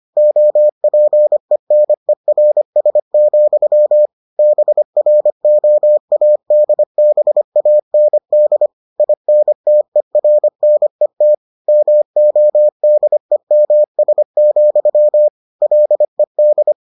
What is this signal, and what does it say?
Morse code, heard as a keyed tone, about 25 wpm, 600 Hz. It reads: OPENERS, BROADBAND INTERNET MODEMS, LED